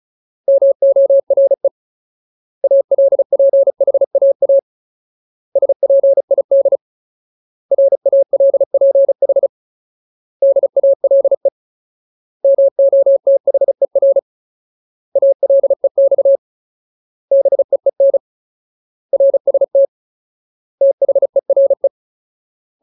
Morse code, heard as a keyed tone, about 34 wpm, 550 Hz. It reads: MORE ALPHAA SPID RALPH DALE MOTHER ALEX BEEN RST THERE
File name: Ultimate Word List / 35WPM Ultimate